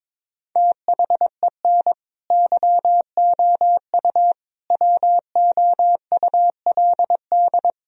Morse code, heard as a keyed tone, about 22 wpm, 700 Hz. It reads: THEN YOU WOULD
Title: Sentences from Top 1000 Words - Review Part 1 - 22wpm